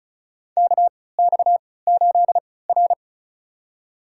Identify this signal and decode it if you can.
Morse code, heard as a keyed tone, about 34 wpm, spaced as for 20 wpm, 700 Hz. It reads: KX8R
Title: USA Call Signs - 4 Characters - 3x Character Spacing - 35wpm